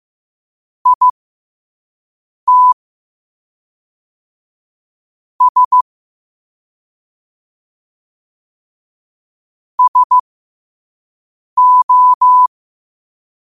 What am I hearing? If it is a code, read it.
Morse code, heard as a keyed tone, about 15 wpm, spaced as for 5 wpm, 1000 Hz. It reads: IT S SO
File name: qotdfiles / 5wpm